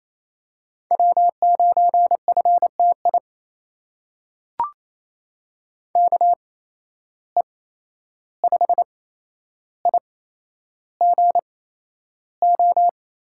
Morse code, heard as a keyed tone, about 28 wpm, spaced as for 11 wpm, 700 Hz. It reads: W9FTI KE5IGO
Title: USA Call Signs - 8x Character Spacing - 28wpm